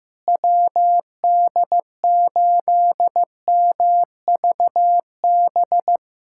Morse code, heard as a keyed tone, about 15 wpm, 700 Hz. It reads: WD8MVB
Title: USA Callsign DE Callsign 15wpm